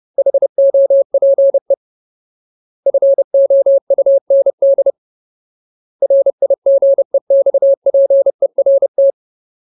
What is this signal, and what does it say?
Morse code, heard as a keyed tone, about 30 wpm, 550 Hz. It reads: HOPE FOUND RIGEXPERT
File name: Ultimate Word List / 30WPM Ultimate